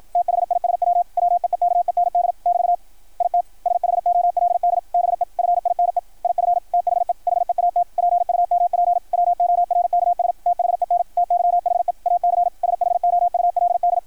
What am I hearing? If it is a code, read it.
Morse code, heard as a keyed tone, about 53 wpm, 660 Hz. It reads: THISW WEEKEND <BT> IT SHOLD BE FINE IF THE HEAT PUMP WORKS THEN TYHE AC SHOULD
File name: archived / CRSnet